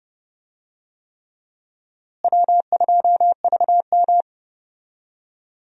Morse code, heard as a keyed tone, about 30 wpm, 700 Hz. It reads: W2VM
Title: USA Call Signs - 4 Characters - Repeated 3x - 30wpm